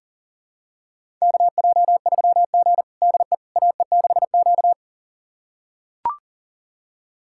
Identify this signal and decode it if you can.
Morse code, heard as a keyed tone, about 40 wpm, 700 Hz. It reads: KJ3G DE AE6Q
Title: USA Callsign DE Callsign - 4 Characters - 40wpm